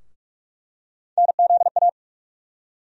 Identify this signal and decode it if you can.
Morse code, heard as a keyed tone, about 46 wpm, 700 Hz. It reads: NZA
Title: USA Call Sign Postfixes - 3 Letters 45wpm